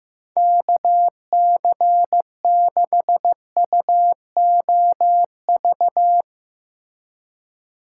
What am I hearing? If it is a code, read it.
Morse code, heard as a keyed tone, about 15 wpm, 700 Hz. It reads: KC6UOV
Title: USA Call Signs - 3x Character Spacing - 15wpm